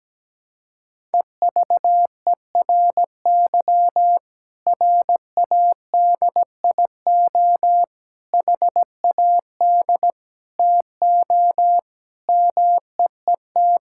Morse code, heard as a keyed tone, about 17 wpm, 700 Hz. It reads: EVERY RADIO HAD TO MEET
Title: Sentences from Top 1000 Words - Review Part 6 - 17wpm